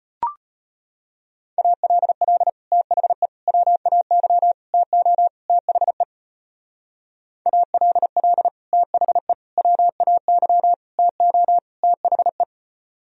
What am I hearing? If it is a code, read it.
Morse code, heard as a keyed tone, about 29 wpm, 700 Hz. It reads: ALL THE WAY TO THE ALL THE WAY TO THE
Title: Speed-Racing - Sets of 5 Words 25wpm